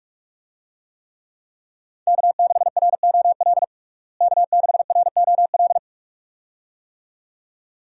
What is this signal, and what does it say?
Morse code, heard as a keyed tone, about 42 wpm, 700 Hz. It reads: K6ROL K6ROL
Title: USA Call Signs - 5 Characters - Repeated 2x - 45wpm